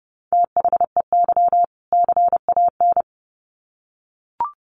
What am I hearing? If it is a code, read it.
Morse code, heard as a keyed tone, about 30 wpm, 700 Hz. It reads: THEY CAN
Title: Sets of 2 Words 30wpm